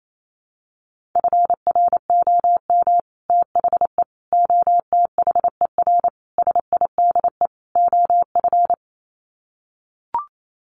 Morse code, heard as a keyed tone, about 28 wpm, 700 Hz. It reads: FROM THE OTHER SIDE OF
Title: Sets of 5 Words Encore 28wpm